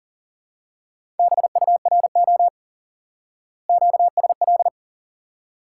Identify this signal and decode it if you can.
Morse code, heard as a keyed tone, about 37 wpm, 700 Hz. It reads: BURO QSL
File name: course / instant-qso-element-course-3 / IQE Course 3 - Lesson 89 - YAGI - Two Elements - 40wpm